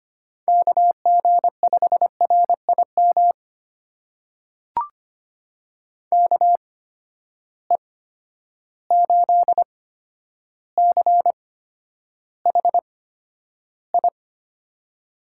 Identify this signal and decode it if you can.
Morse code, heard as a keyed tone, about 25 wpm, spaced as for 11 wpm, 700 Hz. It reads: KG5RIM KE8CHI
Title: USA Call Signs - 6 Characters - 8x Character Spacing - 25wpm